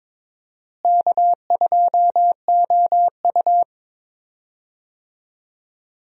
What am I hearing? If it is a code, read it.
Morse code, heard as a keyed tone, about 22 wpm, 700 Hz. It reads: K2OU
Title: USA Call Signs - 4 Characters - Repeated 2x - 22wpm